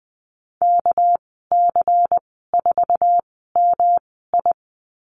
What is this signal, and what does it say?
Morse code, heard as a keyed tone, about 20 wpm, spaced as for 15 wpm, 700 Hz. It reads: KC4MI
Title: 200 Lesson 32g - Letter k - Call Signs 20wpm